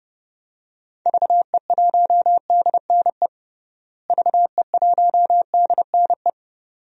Morse code, heard as a keyed tone, about 30 wpm, 700 Hz. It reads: VE1DNE VE1DNE
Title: DXCC-100 Call Signs - 6 Characters - Repeated 2x - 30wpm